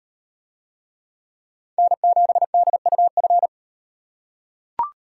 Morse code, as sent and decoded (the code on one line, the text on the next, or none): -. --... -.. ..- ..-.
N7DUF